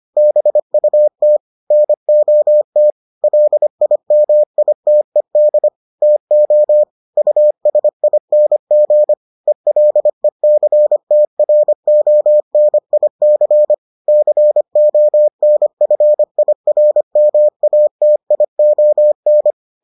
Morse code, heard as a keyed tone, about 25 wpm, 600 Hz. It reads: BUT NOT LIMITED TO USING ELECTRONIC CONFIRMATION